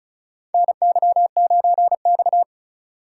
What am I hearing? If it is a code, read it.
Morse code, heard as a keyed tone, about 36 wpm, 700 Hz. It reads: NY9X
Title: K1USN SST QSO Practice - 35wpm